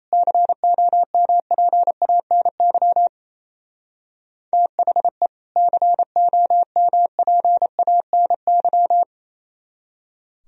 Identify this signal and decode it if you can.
Morse code, heard as a keyed tone, about 33 wpm, 700 Hz. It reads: COMPANY THE COMPANY
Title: Speed-Racing - Sets of 2 Words 28wpm